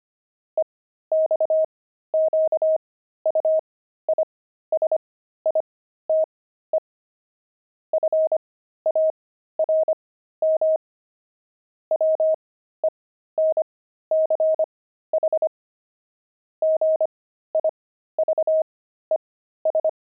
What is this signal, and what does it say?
Morse code, heard as a keyed tone, about 25 wpm, spaced as for 13 wpm, 630 Hz. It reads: EXQUISITE FARM WENCH GIVES
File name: pangram-fs-13-cs-25-jolt